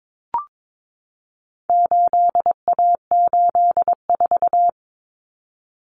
Morse code, heard as a keyed tone, about 22 wpm, 700 Hz. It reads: 8A84
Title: Common Cut Numbers - 4-Digit 22wpm